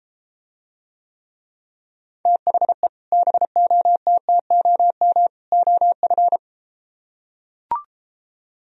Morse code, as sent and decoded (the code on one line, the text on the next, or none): - .... / -... --- - - --- -- / --- ..-.
THE BOTTOM OF